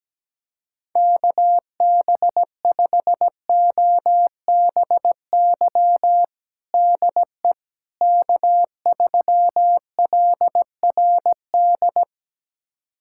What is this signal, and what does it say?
Morse code, heard as a keyed tone, about 17 wpm, 700 Hz. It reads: KB5OBY DE K3LRD
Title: USA Callsign DE Callsign 17wpm